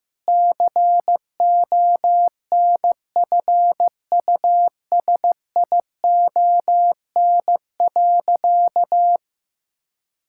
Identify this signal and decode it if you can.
Morse code, heard as a keyed tone, about 15 wpm, 700 Hz. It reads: CONFUSION.
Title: Sentences from Top 1600 Words 15wpm